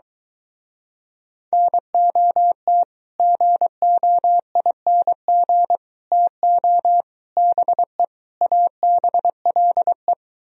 Morse code, heard as a keyed tone, about 23 wpm, 700 Hz. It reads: NOT GOING TO BE ABLE